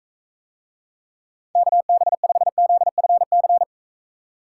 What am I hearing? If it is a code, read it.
Morse code, heard as a keyed tone, about 43 wpm, 700 Hz. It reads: KB5ZFC